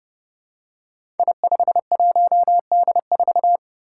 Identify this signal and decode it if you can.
Morse code, heard as a keyed tone, about 30 wpm, 700 Hz. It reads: I51D4
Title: Code Groups - 5 Letter:Numbers - 30wpm